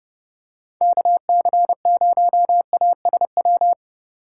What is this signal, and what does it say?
Morse code, heard as a keyed tone, about 30 wpm, 700 Hz. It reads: KC0ASW
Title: USA Call Signs - 6 Characters - 2x Character Spacing - 30wpm